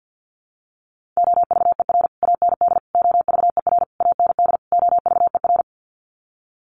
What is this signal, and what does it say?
Morse code, heard as a keyed tone, about 53 wpm, 700 Hz. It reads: OVER AND OVER AND OVER
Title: Sets of 5 Words Encore 50wpm